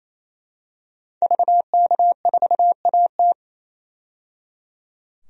Morse code, heard as a keyed tone, about 28 wpm, 700 Hz. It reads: VK4AT
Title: DXCC-100 Call Signs - 5 Characters - 28wpm